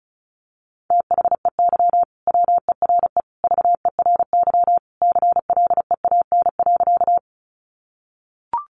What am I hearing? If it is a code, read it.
Morse code, heard as a keyed tone, about 34 wpm, 700 Hz. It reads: THEY WERE VERY CLEAN.
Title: Sentences from Top 1100 Words 35wpm